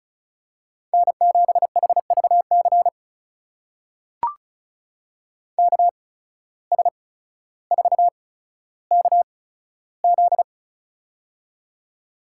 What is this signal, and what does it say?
Morse code, heard as a keyed tone, about 36 wpm, spaced as for 15 wpm, 700 Hz. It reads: N7HVC KS4KZ
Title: USA Call Signs - 5 Characters - 8x Character Spacing - 35wpm